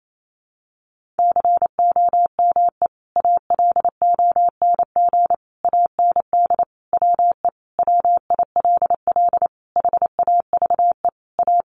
Morse code, heard as a keyed tone, about 28 wpm, 700 Hz. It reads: COME ALONG AND WE WILL HAVE A
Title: Sentences from Top 1500 Words 28wpm